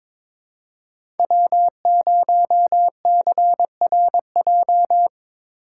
Morse code, heard as a keyed tone, about 22 wpm, 700 Hz. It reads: W0CRJ
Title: USA Call Signs - 5 Characters - 2x Character Spacing - 22wpm